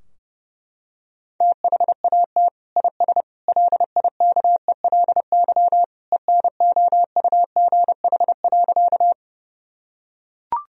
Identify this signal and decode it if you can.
Morse code, heard as a keyed tone, about 30 wpm, 700 Hz. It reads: THAT IS LIKELY ENOUGH.
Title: Sentences from Top 1000 Words - Review Part 4 - 30wpm